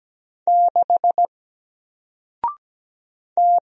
Morse code, heard as a keyed tone, about 17 wpm, 700 Hz. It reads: 6 T
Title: Single Letter-Number Alt 17wpm